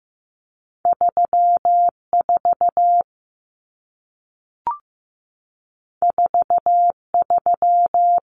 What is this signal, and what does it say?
Morse code, heard as a keyed tone, about 15 wpm, 700 Hz. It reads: 3 4 4 3
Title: Common Cut Numbers - 2-Digit 15wpm